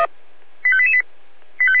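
Five-tone selective call (ZVEI), heard as quick runs of five tones, tones of about 70 ms. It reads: 85888
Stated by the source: ZVEI five-tone selective call received by SDR.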